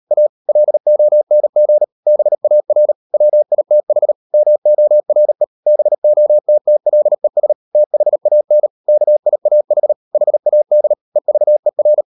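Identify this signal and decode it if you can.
Morse code, heard as a keyed tone, about 38 wpm, 600 Hz. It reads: A LONG BAR WITH MORE BOTTLES THAN KIAH HAD EVER
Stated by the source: Morse code practice at 38 WPM, sending Ken Follett's novel Never in English, from CW med Gnister.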